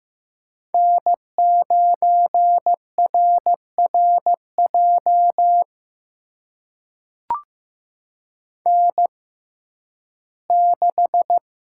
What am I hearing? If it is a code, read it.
Morse code, heard as a keyed tone, about 15 wpm, spaced as for 11 wpm, 700 Hz. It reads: N9RRJ N6
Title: USA Call Signs - 6x Character Spacing - 15wpm